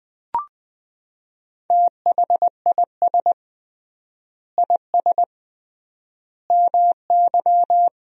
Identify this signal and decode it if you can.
Morse code, heard as a keyed tone, about 19 wpm, 700 Hz. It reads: THIS IS MY
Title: 3x Word-Spacing - Sentences from Top 400 Words - 20wpm